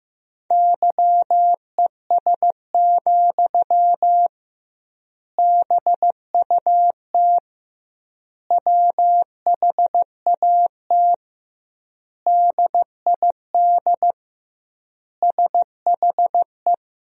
Morse code, heard as a keyed tone, about 16 wpm, 700 Hz. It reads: YES, BUT WHAT DID SHE
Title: 2x Word-Spacing - Sentences from Top 500 Words - 15wpm